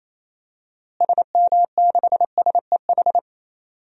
Morse code, heard as a keyed tone, about 28 wpm, 700 Hz. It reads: SM6SEH
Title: DXCC-100 Call Signs - 6 Characters - Repeated 2x - 28wpm